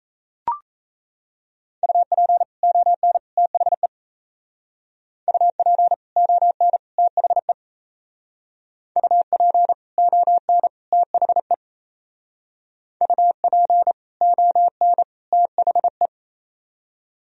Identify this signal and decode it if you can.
Morse code, heard as a keyed tone, about 37 wpm, 700 Hz. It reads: UP ON THE UP ON THE UP ON THE UP ON THE